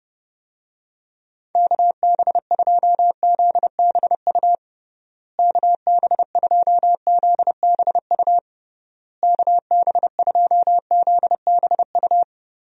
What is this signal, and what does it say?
Morse code, heard as a keyed tone, about 30 wpm, 700 Hz. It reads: KB2ZBU KB2ZBU KB2ZBU
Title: USA Call Signs - 6 Characters - Repeated 3x - 30wpm